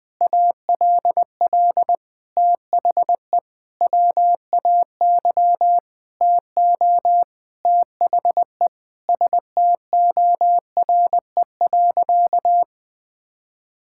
Morse code, heard as a keyed tone, about 21 wpm, 700 Hz. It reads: ALL THE WAY TO THE STORE.
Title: Sentences with 5-Word Phrases - 20wpm